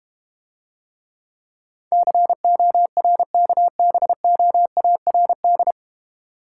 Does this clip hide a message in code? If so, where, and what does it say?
Yes: Morse code from 1.9 to 5.7 s, heard as a keyed tone, about 32 wpm, 700 Hz. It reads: CORKBOARD